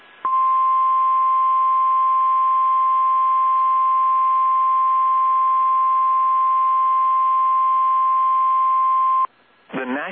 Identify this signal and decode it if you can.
Emergency Alert System (SAME) tones (the 1050 Hz attention tone). It is heard from 0.2 to 9.3 s.